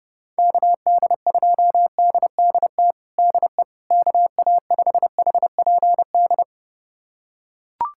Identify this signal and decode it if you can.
Morse code, heard as a keyed tone, about 30 wpm, 700 Hz. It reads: KD2DDT DE KA5HPD
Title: USA Callsign DE Callsign - 6 Characters - 30wpm